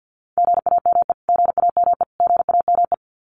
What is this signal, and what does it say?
Morse code, heard as a keyed tone, about 50 wpm, 700 Hz. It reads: GAME GAME GAME
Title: Top 500 Words Repeated 50wpm